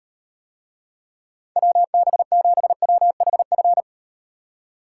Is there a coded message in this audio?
Yes: Morse code from 1.6 to 3.8 s, heard as a keyed tone, about 38 wpm, 700 Hz. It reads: WB7WHF